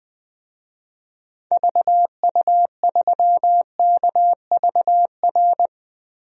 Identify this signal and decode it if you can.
Morse code, heard as a keyed tone, about 20 wpm, 700 Hz. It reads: VU3KVR
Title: DXCC-100 Call Signs - 6 Characters - Repeated 3x - 20wpm